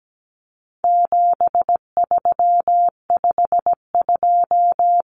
Morse code, heard as a keyed tone, about 17 wpm, 700 Hz. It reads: 7352
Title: Numbers - 4-Digit 17wpm